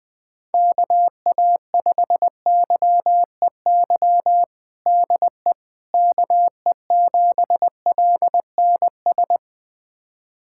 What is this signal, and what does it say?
Morse code, heard as a keyed tone, about 21 wpm, 700 Hz. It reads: KA5YEY DE KE7LNS
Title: USA Callsign DE Callsign 20wpm